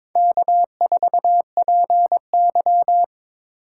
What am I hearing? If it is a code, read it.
Morse code, heard as a keyed tone, about 26 wpm, 700 Hz. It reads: K4PY
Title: Speed Racing - USA 1x2 Call Signs 22wpm